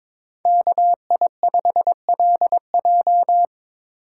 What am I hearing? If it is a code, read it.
Morse code, heard as a keyed tone, about 25 wpm, 700 Hz. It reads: KI5LJ
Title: Speed Racing - USA 2x2 Call Signs 22wpm